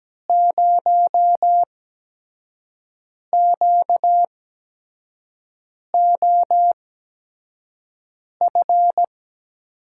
Morse code, heard as a keyed tone, about 17 wpm, spaced as for 5 wpm, 700 Hz. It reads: 0QOF
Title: USA Call Signs - 6 Characters - 8x Character Spacing - 17wpm